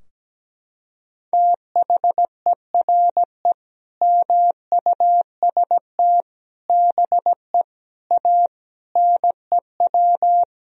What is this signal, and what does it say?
Morse code, heard as a keyed tone, about 17 wpm, 700 Hz. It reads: THERE MUST BE A NEW
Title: Sentences from Top 1000 Words - Review Part 3 - 17wpm